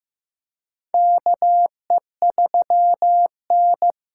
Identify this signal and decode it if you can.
Morse code, heard as a keyed tone, about 15 wpm, 700 Hz. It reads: KE3N
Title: USA Call Signs - 4 Characters - 2x Character Spacing - 15wpm